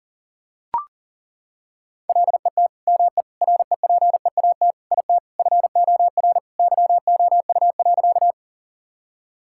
Morse code, heard as a keyed tone, about 40 wpm, 700 Hz. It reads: LET ME REPEAT IT FOR YOU.
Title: Sentences from Top 1600 Words 40wpm